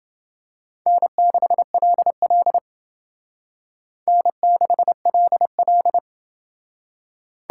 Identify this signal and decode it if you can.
Morse code, heard as a keyed tone, about 29 wpm, 700 Hz. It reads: N6LL N6LL